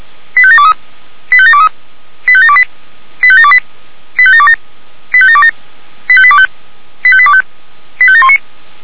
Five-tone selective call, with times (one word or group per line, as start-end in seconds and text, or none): 0.4-0.7 85723
1.3-1.7 85723
2.3-2.6 85728
3.2-3.6 85728
4.2-4.6 85727
5.1-5.5 85727
6.1-6.5 85825
7.0-7.4 85825
8.0-8.4 85719